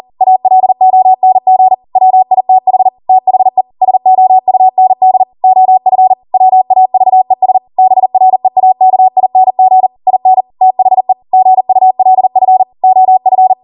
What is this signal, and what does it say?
Morse code, heard as a keyed tone, about 40 wpm, 750 Hz. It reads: ALONG WITH THE SOUND OF WAVES BREAKING IN THE GULF OF